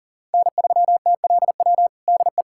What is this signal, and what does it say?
Morse code, heard as a keyed tone, about 40 wpm, 700 Hz. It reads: N3TLW DE
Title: USA Callsign DE Callsign - 5 Characters - 40wpm